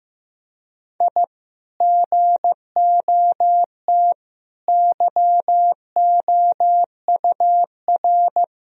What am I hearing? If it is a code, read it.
Morse code, heard as a keyed tone, about 15 wpm, 700 Hz. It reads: I GOT YOUR
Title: Sentences from Top 1000 Words - Review Part 2 - 15wpm